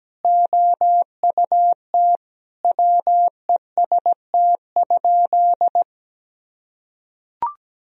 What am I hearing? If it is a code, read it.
Morse code, heard as a keyed tone, about 17 wpm, 700 Hz. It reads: OUT WEST?
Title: Sentences from Top 1000 Words - Review Part 2 - 17wpm